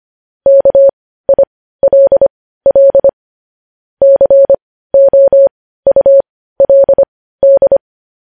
Morse code, heard as a keyed tone, about 25 wpm, spaced as for 15 wpm, 550 Hz. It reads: KILL COULD
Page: news-headlines-in-morse-code-at-15-wpm